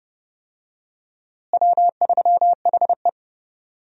Morse code, heard as a keyed tone, about 30 wpm, 700 Hz. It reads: W3HE
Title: USA Call Signs - 4 Characters - Repeated 3x - 30wpm